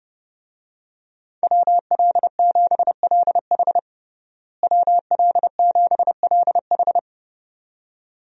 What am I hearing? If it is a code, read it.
Morse code, heard as a keyed tone, about 32 wpm, 700 Hz. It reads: WL7LH WL7LH
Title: USA Call Signs - 5 Characters - Repeated 2x - 30wpm